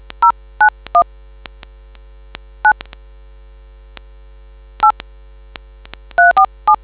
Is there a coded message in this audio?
Yes: DTMF from 0.2 to 6.8 s, heard as telephone touch tones, tones of about 100 ms. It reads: 091 9 8 347